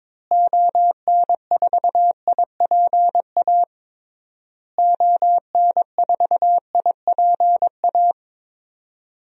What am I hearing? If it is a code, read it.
Morse code, heard as a keyed tone, about 22 wpm, 700 Hz. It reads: ON4IPA ON4IPA
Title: DXCC-100 Call Signs - 6 Characters - Repeated 3x - 22wpm